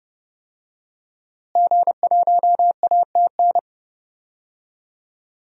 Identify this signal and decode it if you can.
Morse code, heard as a keyed tone, about 30 wpm, 700 Hz. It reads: G1ATN
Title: DXCC-100 Call Signs - 5 Characters - Repeated 3x - 30wpm